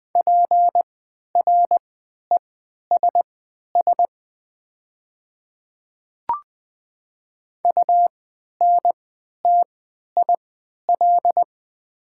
Morse code, heard as a keyed tone, about 21 wpm, spaced as for 12 wpm, 700 Hz. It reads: PRESS UNTIL
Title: 3x Character Spacing - Top 300 Words - 20wpm